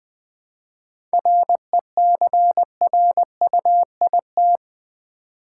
Morse code, heard as a keyed tone, about 20 wpm, 700 Hz. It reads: RECRUIT
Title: 7 Letter Words - First 3 Letters Given - 20wpm